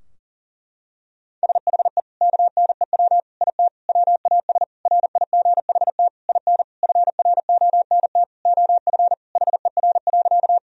Morse code, heard as a keyed tone, about 40 wpm, 700 Hz. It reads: SHE KNEW IT WAS RIGHT IN FRONT OF HER.
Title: Sentences from Top 1000 Words - Review Part 2 - 40wpm